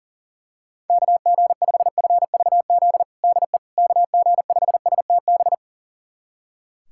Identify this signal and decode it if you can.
Morse code, heard as a keyed tone, about 40 wpm, 700 Hz. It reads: KG5FVZ DE KG5STB
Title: USA Callsign DE Callsign - 6 Characters - 40wpm